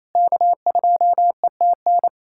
Morse code, heard as a keyed tone, about 28 wpm, 700 Hz. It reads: K2ETN
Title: USA Callsign DE Callsign - 5 Characters - 28wpm